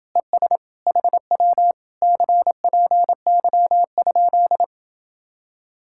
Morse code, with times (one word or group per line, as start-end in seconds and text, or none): 0.2-0.6 ES
0.9-1.7 HW
2.0-4.6 CPY?